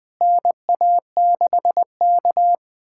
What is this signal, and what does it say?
Morse code, heard as a keyed tone, about 21 wpm, 700 Hz. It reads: NA6K
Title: USA Callsign DE Callsign - 4 Characters - 20wpm